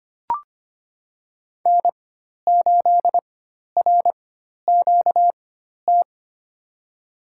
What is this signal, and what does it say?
Morse code, heard as a keyed tone, about 25 wpm, spaced as for 12 wpm, 700 Hz. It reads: N8RQT
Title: USA Call Signs - 5 Characters - 4x Character Spacing - 25wpm